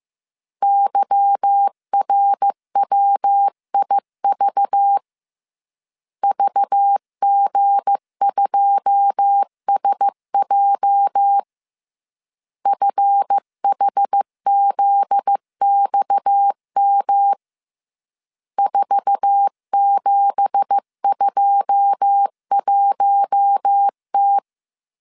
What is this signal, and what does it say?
Morse code, heard as a keyed tone, about 15 wpm, 800 Hz. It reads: YRWIV VG2SJ FHZXM 4721T
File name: CW / DET / 1 9 1500C S C 14wpm 3245